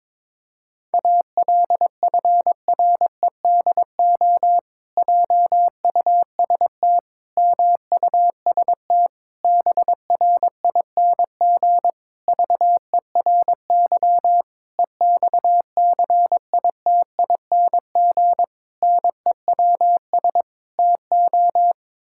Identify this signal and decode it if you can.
Morse code, heard as a keyed tone, about 22 wpm, 700 Hz. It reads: ALFREDO JUST MUST BRING VERY EXCITING NEWS TO
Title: Pangram Challenge v2 - 22wpm